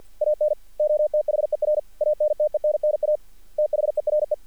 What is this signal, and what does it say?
Morse code, heard as a keyed tone, about 48 wpm, 600 Hz. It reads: ANOTHER ANTENNA THERE
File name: archived / CRSnet